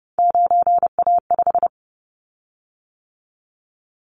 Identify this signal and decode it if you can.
Morse code, heard as a keyed tone, about 30 wpm, 700 Hz. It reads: 9A5
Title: Callsign Prefixes for Top 50 DXCC - 30wpm